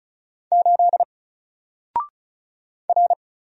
Morse code, heard as a keyed tone, about 34 wpm, 700 Hz. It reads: 8 R
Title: Single Letter-Number Alt 35wpm